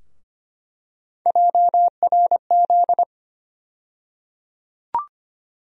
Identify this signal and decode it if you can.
Morse code, heard as a keyed tone, about 25 wpm, 700 Hz. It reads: JRZ
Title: USA Call Sign Postfixes - 3 Letters 25wpm